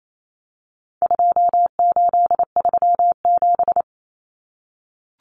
Morse code, heard as a keyed tone, about 28 wpm, 700 Hz. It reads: 2837
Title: Common Cut Numbers - 4-Digit 28wpm